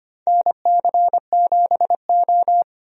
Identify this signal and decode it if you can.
Morse code, heard as a keyed tone, about 25 wpm, 700 Hz. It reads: NC7O
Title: USA Call Signs - 4 Characters - Repeated 3x - 25wpm